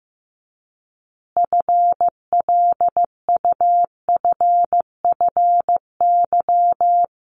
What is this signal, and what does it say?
Morse code, heard as a keyed tone, about 15 wpm, 700 Hz. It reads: FLUFFY
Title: Easy Phrases 15wpm